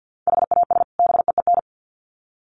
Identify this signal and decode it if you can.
Morse code, heard as a keyed tone, about 50 wpm, 700 Hz. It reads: HAS BEEN
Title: Sets of 2 Words 50wpm